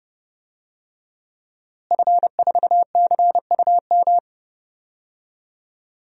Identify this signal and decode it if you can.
Morse code, heard as a keyed tone, about 30 wpm, 700 Hz. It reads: F4CUM
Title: DXCC-100 Call Signs - 5 Characters - Repeated 2x - 30wpm